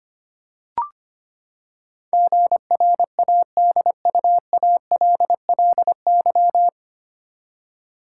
Morse code, heard as a keyed tone, about 25 wpm, 700 Hz. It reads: GRADUALLY
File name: Nine Letter Words - 25wpm